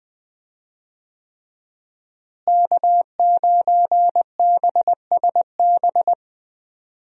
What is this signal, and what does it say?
Morse code, heard as a keyed tone, about 20 wpm, 700 Hz. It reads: K9BSB
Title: USA Call Signs - 5 Characters - Repeated 3x - 20wpm